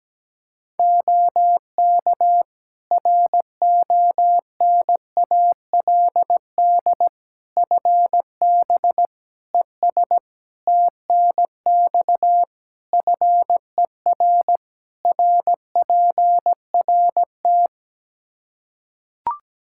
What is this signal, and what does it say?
Morse code, heard as a keyed tone, about 17 wpm, 700 Hz. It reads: OK RONALD FB ES TNX FER RPRT
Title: Practice QSO - Exchange 2 - Info - 17wpm